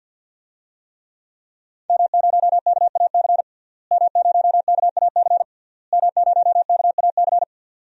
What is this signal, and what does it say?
Morse code, heard as a keyed tone, about 46 wpm, 700 Hz. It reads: M0KAC M0KAC M0KAC
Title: DXCC-100 Call Signs - 5 Characters - Repeated 3x - 50wpm